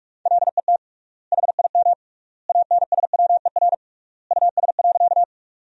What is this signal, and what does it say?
Morse code, heard as a keyed tone, about 41 wpm, 700 Hz. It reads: LET HIM ANSWER US.
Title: 3x Word-Spacing - Sentences from Top 300 Words - 45wpm